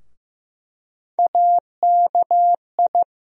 Morse code, heard as a keyed tone, about 15 wpm, 700 Hz. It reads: AKI
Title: USA Call Sign Postfixes - 3 Letters 15wpm